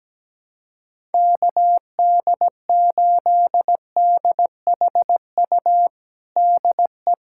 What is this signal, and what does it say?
Morse code, heard as a keyed tone, about 17 wpm, 700 Hz. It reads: KD8DHU DE
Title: USA Callsign DE Callsign 17wpm